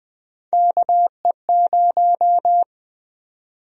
Morse code, heard as a keyed tone, about 20 wpm, 700 Hz. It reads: KE0
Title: USA Call Sign Prefixes - 20wpm